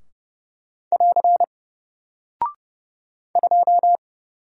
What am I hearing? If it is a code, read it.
Morse code, heard as a keyed tone, about 30 wpm, 700 Hz. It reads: <AR> 2
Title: Single Character (Letter:Number:Punctuation:Prosign) Alt 30wpm